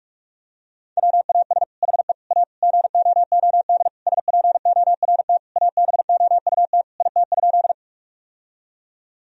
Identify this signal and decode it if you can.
Morse code, heard as a keyed tone, about 43 wpm, 700 Hz. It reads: WAS HE A GOOD SPORT ABOUT IT?
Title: Sentences from Top 1000 Words - Review Part 4 - 45wpm